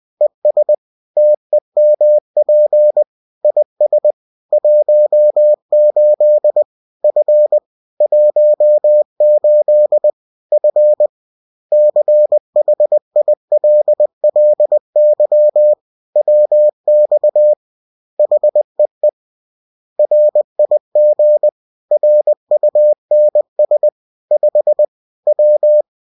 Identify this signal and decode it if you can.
Morse code, heard as a keyed tone, about 20 wpm, 600 Hz. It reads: ES TEMP IS 18 F 18 F CHILLY WX HEE RIG RUNS 5 W